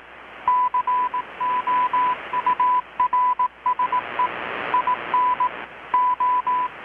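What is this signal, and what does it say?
Morse code, heard as a keyed tone, about 18 wpm, 1000 Hz, with radio noise. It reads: COURSE IN O
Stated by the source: Morse from KPH's Night of Nights XX maritime radiotelegraphy event, received in Chiapas, Mexico about 3,800 km away on a Kenwood R-600 with a grounded 10 m long-wire antenna.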